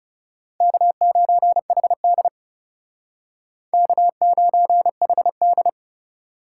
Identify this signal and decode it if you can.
Morse code, heard as a keyed tone, about 38 wpm, 700 Hz. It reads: K9HD K9HD